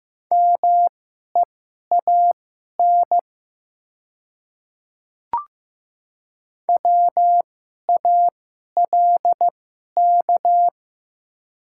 Morse code, heard as a keyed tone, about 15 wpm, spaced as for 11 wpm, 700 Hz. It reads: MEAN WALK
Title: 2x Character Spacing - Top 300 Words - 15wpm